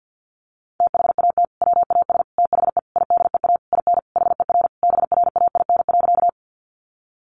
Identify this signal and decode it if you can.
Morse code, heard as a keyed tone, about 50 wpm, 700 Hz. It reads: THAT WAS THE IDEA IN HER BRAIN.
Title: Sentences from Top 1100 Words 50wpm